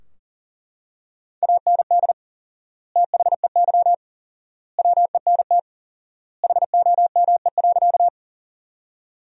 Morse code, heard as a keyed tone, about 37 wpm, 700 Hz. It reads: AND THEY WENT HOME.
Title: 4x Word-Spacing - Sentences from Top 200 Words - 40wpm